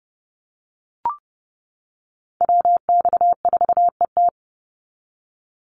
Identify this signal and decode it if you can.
Morse code, heard as a keyed tone, about 30 wpm, 700 Hz. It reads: WX4ET
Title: NA SOTA Chaser Callsigns 30wpm